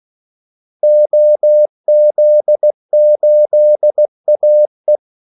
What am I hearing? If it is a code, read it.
Morse code, heard as a keyed tone, about 16 wpm, 600 Hz. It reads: OZ8AE
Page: Kaldesignaler 16 wpm